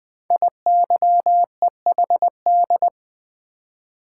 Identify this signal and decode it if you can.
Morse code, heard as a keyed tone, about 20 wpm, 700 Hz. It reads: IYEHD
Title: Code Groups - 5 Letters - 20wpm